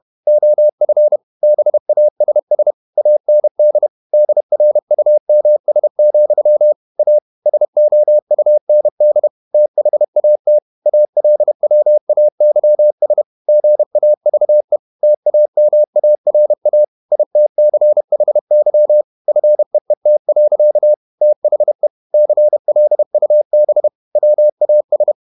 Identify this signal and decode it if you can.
Morse code, heard as a keyed tone, about 31 wpm, 600 Hz. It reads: OF BASS AND DRUMS, A SOUND THAT ALWAYS GAVE TAMARA ITCHY FEET. THE CLUB WAS
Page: Never 31wpm | CW med Gnister